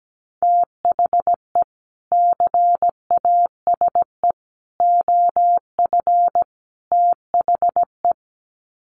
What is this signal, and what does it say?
Morse code, heard as a keyed tone, about 17 wpm, 700 Hz. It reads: THE CASE OF THE
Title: Sets of 4 Words 17wpm